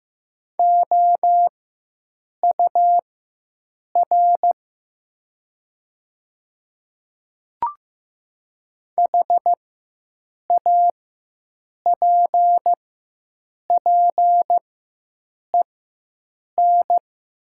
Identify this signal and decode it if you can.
Morse code, heard as a keyed tone, about 16 wpm, spaced as for 7 wpm, 700 Hz. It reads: OUR HAPPEN
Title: 4x Character Spacing - Top 500 Words - 15wpm